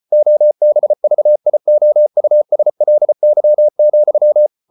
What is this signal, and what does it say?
Morse code, heard as a keyed tone, about 34 wpm, 600 Hz. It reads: OBVIOUSLY,